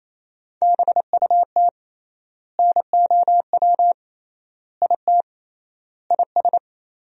Morse code, heard as a keyed tone, about 28 wpm, 700 Hz. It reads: BUT NOW IT IS
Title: 3x Word-Spacing - Sentences from Top 300 Words - 28wpm